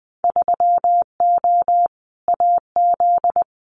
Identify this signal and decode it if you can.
Morse code, heard as a keyed tone, about 20 wpm, 700 Hz. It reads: 3O AZ
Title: Winter Field Day Exchanges 20wpm